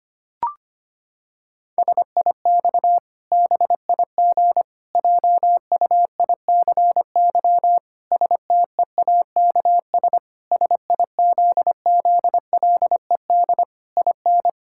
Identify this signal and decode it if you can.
Morse code, heard as a keyed tone, about 25 wpm, 700 Hz. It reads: SIX BIG JUICY STEAKS SIZZLED IN
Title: Pangram Challenge v2 - 25wpm